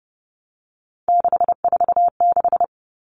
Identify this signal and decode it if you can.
Morse code, heard as a keyed tone, about 30 wpm, 700 Hz. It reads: 646
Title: Numbers - 3-Digit 30wpm